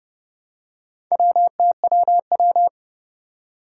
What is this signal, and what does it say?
Morse code, heard as a keyed tone, about 30 wpm, 700 Hz. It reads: WTWW
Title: Four Challenging Letter-Number Combinations 30wpm